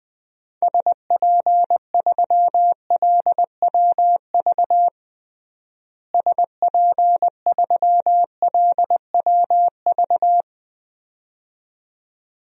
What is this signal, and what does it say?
Morse code, heard as a keyed tone, about 21 wpm, 700 Hz. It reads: SP3LWV SP3LWV
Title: DXCC-100 Call Signs - 6 Characters - Repeated 2x - 20wpm